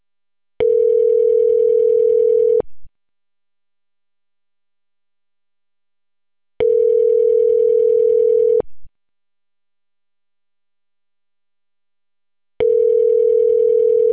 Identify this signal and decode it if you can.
Ringback tone (North America), 440 + 480 Hz.